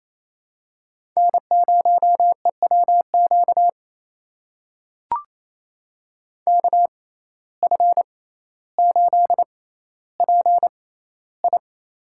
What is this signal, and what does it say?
Morse code, heard as a keyed tone, about 28 wpm, spaced as for 14 wpm, 700 Hz. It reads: N0EWQ KF8PI
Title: USA Call Signs - 5 Characters - 6x Character Spacing - 28wpm